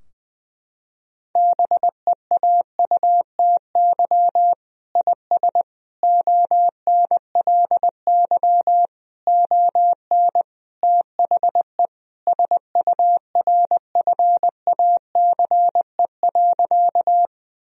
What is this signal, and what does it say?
Morse code, heard as a keyed tone, about 20 wpm, 700 Hz. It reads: BEAUTY IS ONLY ON THE SURFACE.
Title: Sentences from Top 1000 Words - Review Part 3 - 20wpm